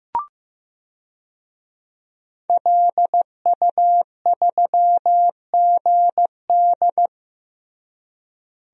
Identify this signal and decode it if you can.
Morse code, heard as a keyed tone, about 15 wpm, 700 Hz. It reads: LU3GD
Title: DXCC-100 Call Signs - 5 Characters - Repeated 2x - 15wpm